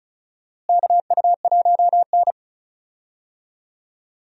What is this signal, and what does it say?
Morse code, heard as a keyed tone, about 36 wpm, 700 Hz. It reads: KU1N
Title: USA Call Signs - 4 Characters - Repeated 2x - 35wpm